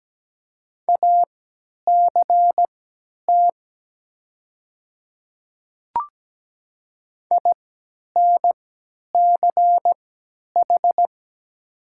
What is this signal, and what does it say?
Morse code, heard as a keyed tone, about 17 wpm, spaced as for 10 wpm, 700 Hz. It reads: ACT INCH
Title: 3x Character Spacing - Top 500 Words - 17wpm